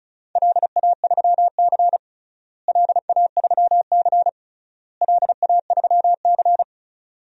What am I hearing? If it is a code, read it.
Morse code, heard as a keyed tone, about 36 wpm, 700 Hz. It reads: LA3C LA3C LA3C
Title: DXCC-100 Call Signs - 4 Characters - Repeated 3x - 35wpm